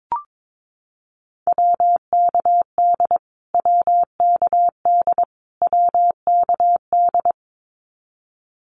Morse code, heard as a keyed tone, about 22 wpm, 700 Hz. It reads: WKD WKD WKD
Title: Abbreviations Repeated 3x - 22wpm